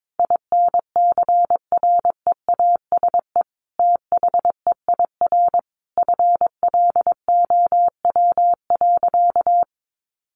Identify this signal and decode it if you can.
Morse code, heard as a keyed tone, about 22 wpm, 700 Hz. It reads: INCREASE THEIR FLOW.
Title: Sentences from Top 1200 Words 22wpm